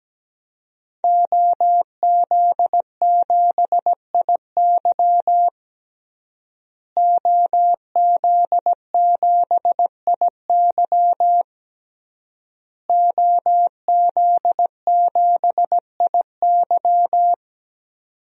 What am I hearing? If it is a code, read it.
Morse code, heard as a keyed tone, about 17 wpm, 700 Hz. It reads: OZ7IY OZ7IY OZ7IY
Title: DXCC-100 Call Signs - 5 Characters - Repeated 3x - 17wpm